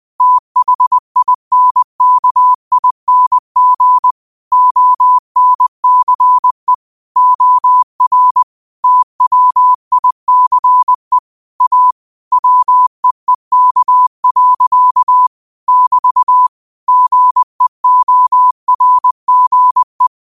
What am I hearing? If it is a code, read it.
Morse code, heard as a keyed tone, about 20 wpm, 1000 Hz. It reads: THINKING ONCE OR TWICE A WEEK. <BT> GEORGE
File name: qotdfiles / 20wpm